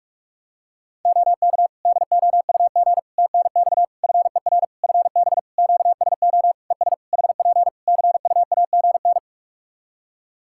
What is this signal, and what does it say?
Morse code, heard as a keyed tone, about 41 wpm, 700 Hz. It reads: OK DOUG TNX FER FB QSO ES HP CUAGN
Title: Practice QSO - Exchange 4 - Ending - 45wpm